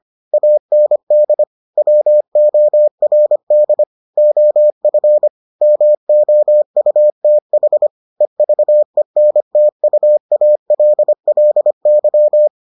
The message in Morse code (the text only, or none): AND WORD OF MOUTH EVENTUALLY